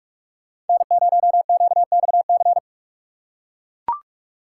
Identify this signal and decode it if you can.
Morse code, heard as a keyed tone, about 44 wpm, 690 Hz. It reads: N0QXC
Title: USA Call Signs - 6x Character Spacing - 45wpm